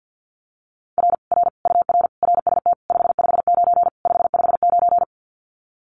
Morse code, heard as a keyed tone, about 50 wpm, 700 Hz. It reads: R R UR RST 559 559
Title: Common QSO Phrases Encore 50wpm